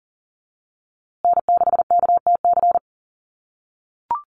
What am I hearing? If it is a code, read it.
Morse code, heard as a keyed tone, about 40 wpm, 700 Hz. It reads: N6KTC
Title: USA Call Signs Repeated 3x 40wpm